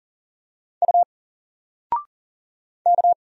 Morse code, heard as a keyed tone, about 40 wpm, 700 Hz. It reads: U K
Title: Single Letter-Number 40wpm